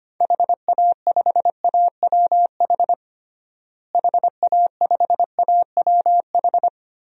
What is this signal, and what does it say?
Morse code, heard as a keyed tone, about 25 wpm, 700 Hz. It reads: HA5AWH HA5AWH
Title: DXCC-100 Call Signs - 6 Characters - Repeated 3x - 25wpm